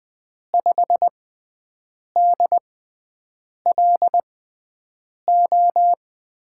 Morse code, heard as a keyed tone, about 20 wpm, spaced as for 7 wpm, 700 Hz. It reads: 5DLO
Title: USA Call Signs - 6x Character Spacing - 20wpm